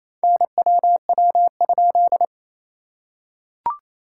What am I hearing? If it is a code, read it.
Morse code, heard as a keyed tone, about 28 wpm, 690 Hz. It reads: NWW?
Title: SOTA QSO Practice 28wpm